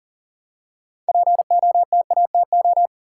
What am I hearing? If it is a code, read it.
Morse code, heard as a keyed tone, about 40 wpm, 700 Hz. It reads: POTATO
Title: Six Letter Words - 40wpm